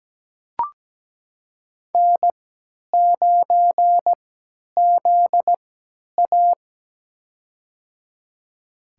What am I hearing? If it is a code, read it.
Morse code, heard as a keyed tone, about 17 wpm, spaced as for 10 wpm, 700 Hz. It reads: N9ZA
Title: USA Call Signs - 3x Character Spacing - 17wpm